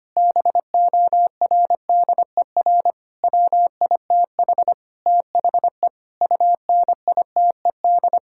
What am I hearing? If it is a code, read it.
Morse code, heard as a keyed tone, about 25 wpm, 700 Hz. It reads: BORDER WITH THE UNITED
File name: practice / 25wpm / Sentences with 4-Word Phrases - 25wpm